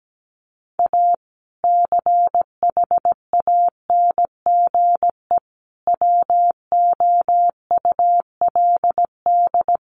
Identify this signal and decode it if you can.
Morse code, heard as a keyed tone, about 17 wpm, 700 Hz. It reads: A CHANGE WOULD
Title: Sentences from Top 1400 Words 17wpm